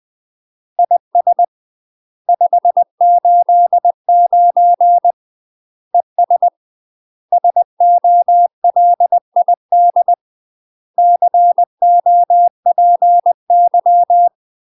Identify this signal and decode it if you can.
Morse code, heard as a keyed tone, about 20 wpm, 700 Hz. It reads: IS 589 ES SOLID COPY
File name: UK MASTER QSOs TRAINING / 20WPM / UK MASTER QSOs 0034